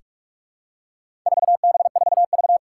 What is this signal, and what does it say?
Morse code, heard as a keyed tone, about 44 wpm, 700 Hz. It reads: 4B4V
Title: Four Challenging Letter-Number Combinations 45wpm